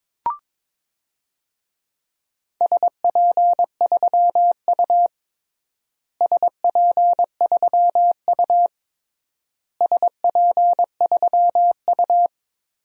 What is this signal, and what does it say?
Morse code, heard as a keyed tone, about 22 wpm, 700 Hz. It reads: SP3U SP3U SP3U
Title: DXCC-100 Call Signs - 4 Characters - Repeated 3x - 22wpm